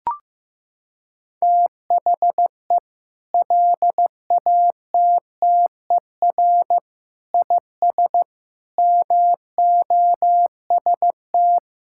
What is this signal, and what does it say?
Morse code, heard as a keyed tone, about 15 wpm, 700 Hz. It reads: THE LATTER IS MOST
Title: Sentences from Top 1600 Words 15wpm